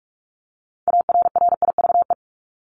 Morse code, heard as a keyed tone, about 44 wpm, 700 Hz. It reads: ARRIVE
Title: ED Suffix - Top 100 Words with Base Words 45wpm